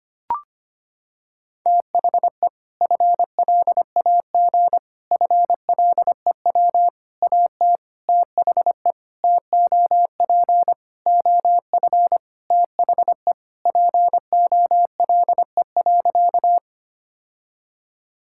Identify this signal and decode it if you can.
Morse code, heard as a keyed tone, about 25 wpm, 700 Hz. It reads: THE FLAG FLEW AT THE TOP OF THE POLE.
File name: practice / 25wpm / Sentences with 4-Word Phrases - 25wpm